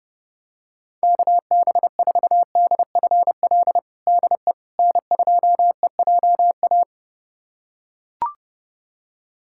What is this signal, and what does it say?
Morse code, heard as a keyed tone, about 30 wpm, 700 Hz. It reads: KB4DFL DE N2EJA
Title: USA Callsign DE Callsign 30wpm